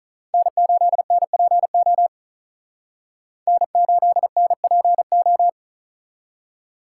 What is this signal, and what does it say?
Morse code, heard as a keyed tone, about 46 wpm, 700 Hz. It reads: N8NPO N8NPO